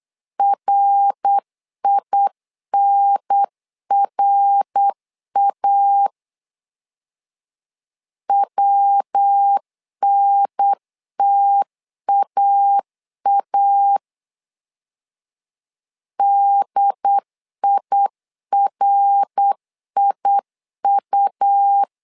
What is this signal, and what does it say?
Morse code, heard as a keyed tone, about 8 wpm, 800 Hz. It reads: RINRA WNTAA DIRIU
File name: LEZIONI / 3 MWRO / 3 MWR 8wpm 2444